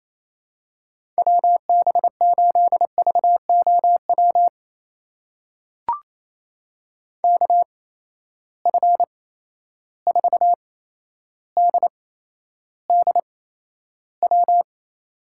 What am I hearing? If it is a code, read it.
Morse code, heard as a keyed tone, about 28 wpm, spaced as for 12 wpm, 700 Hz. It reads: WB8VOW KF4DDW
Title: USA Call Signs - 6 Characters - 8x Character Spacing - 28wpm